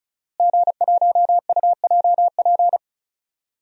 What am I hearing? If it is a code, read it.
Morse code, heard as a keyed tone, about 36 wpm, 700 Hz. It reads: G1UJP
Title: DXCC-100 Call Signs - 5 Characters - Repeated 3x - 35wpm